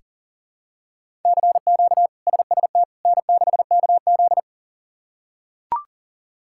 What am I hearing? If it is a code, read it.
Morse code, heard as a keyed tone, about 40 wpm, 700 Hz. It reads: CQ SST N6KZ
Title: K1USN SST QSO Practice - 40wpm